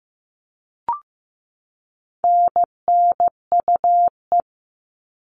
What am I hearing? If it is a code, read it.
Morse code, heard as a keyed tone, about 15 wpm, 700 Hz. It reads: NNUE
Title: All Cut Numbers - 4-Digit 15wpm